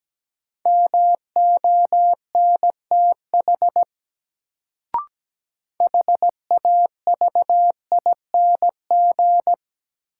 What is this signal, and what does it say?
Morse code, heard as a keyed tone, about 17 wpm, 700 Hz. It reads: MONTH HAVING
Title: Sets of 5 Words Encore - New Word Drill - 17wpm